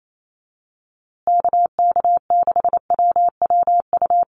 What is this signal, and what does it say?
Morse code, heard as a keyed tone, about 28 wpm, 700 Hz. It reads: KK6WWU
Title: USA Call Signs Repeated 3x 28wpm